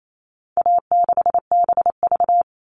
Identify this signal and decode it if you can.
Morse code, heard as a keyed tone, about 28 wpm, 700 Hz. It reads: A6BV
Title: All Cut Numbers - 4-Digit 28wpm